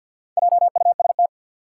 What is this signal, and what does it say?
Morse code, heard as a keyed tone, about 53 wpm, 700 Hz. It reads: JUST
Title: Four Letter Words - 50wpm